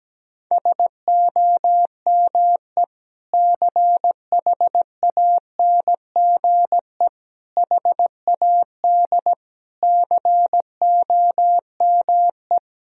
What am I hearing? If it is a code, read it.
Morse code, heard as a keyed tone, about 17 wpm, 700 Hz. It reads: SOME CHANGE HAD COME
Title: Sentences from Top 1000 Words - Review Part 2 - 17wpm